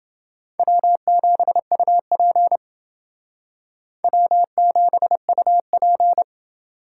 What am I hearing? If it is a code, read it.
Morse code, heard as a keyed tone, about 28 wpm, 700 Hz. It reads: W7UP W7UP